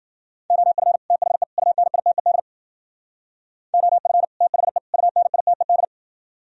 Morse code, heard as a keyed tone, about 52 wpm, 700 Hz. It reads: OF THE UNITED OF THE UNITED